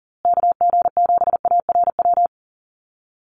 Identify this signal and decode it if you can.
Morse code, heard as a keyed tone, about 40 wpm, 700 Hz. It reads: KG7ARW
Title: USA Call Signs Repeated 3x 40wpm